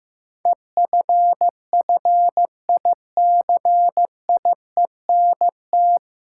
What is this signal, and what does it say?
Morse code, heard as a keyed tone, about 15 wpm, 700 Hz. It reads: EFFICIENT
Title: Sentences from Top 1300 Words - New Word Drill - 15wpm